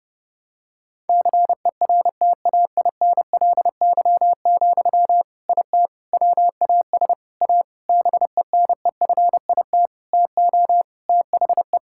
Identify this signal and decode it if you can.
Morse code, heard as a keyed tone, about 32 wpm, 700 Hz. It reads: CERTAINLY, IT WAS A BENEFIT TO THE
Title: Sentences from Top 1000 Words - Review Part 6 - 30wpm